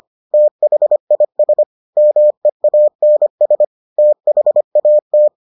Morse code, heard as a keyed tone, about 25 wpm, 600 Hz. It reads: THIS MEANS THAT